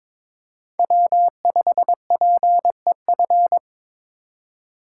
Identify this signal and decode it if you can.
Morse code, heard as a keyed tone, about 26 wpm, 700 Hz. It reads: W5PEF
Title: Speed Racing - USA Call Signs 22wpm